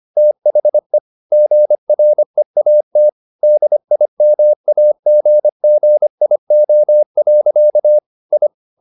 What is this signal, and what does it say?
Morse code, heard as a keyed tone, about 25 wpm, 600 Hz. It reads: THE GREAT DIMAGGIO. I
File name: the old man and the sea part 1 25wpm